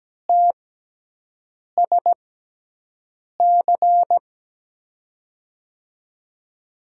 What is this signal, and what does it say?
Morse code, heard as a keyed tone, about 17 wpm, spaced as for 6 wpm, 700 Hz. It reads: TSC
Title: USA Call Signs - 6 Characters - 6x Character Spacing - 17wpm